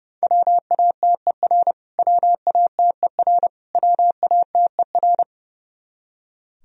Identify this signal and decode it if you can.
Morse code, heard as a keyed tone, about 30 wpm, 700 Hz. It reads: WATER WATER WATER
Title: Top 100 Words Repeated 30wpm